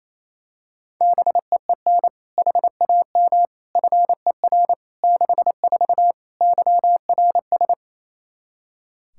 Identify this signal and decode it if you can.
Morse code, heard as a keyed tone, about 28 wpm, 700 Hz. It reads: BEEN HAM FER 64 YRS
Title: K1USN SST QSO Practice 28wpm